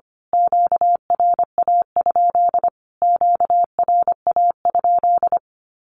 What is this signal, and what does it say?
Morse code, heard as a keyed tone, about 25 wpm, 700 Hz. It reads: QRA? QRA?
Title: Q-Codes Repeated 3x 25wpm